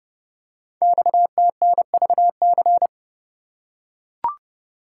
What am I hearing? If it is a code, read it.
Morse code, heard as a keyed tone, about 30 wpm, 700 Hz. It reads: XTNVC
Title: Code Groups - 5 Letter:Numbers - 30wpm